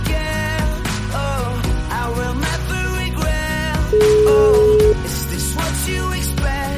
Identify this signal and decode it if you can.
Ringback tone (Europe, 425 Hz), with noise.